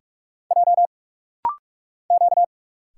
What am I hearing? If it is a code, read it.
Morse code, heard as a keyed tone, about 46 wpm, 690 Hz. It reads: J Q
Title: Single Letters - Rapid-Fire 45wpm